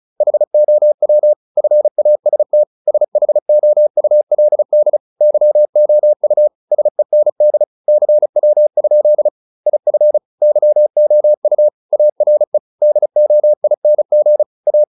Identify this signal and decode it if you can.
Morse code, heard as a keyed tone, about 34 wpm, 600 Hz. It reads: HOW FAST SHOULD YOU SEND CW? IF YOU ARE DOING A